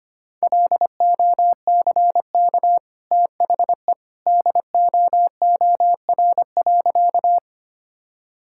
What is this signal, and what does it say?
Morse code, heard as a keyed tone, about 25 wpm, 700 Hz. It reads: LOCK THE DOOR.
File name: Easy Sentences with Hints - 25wpm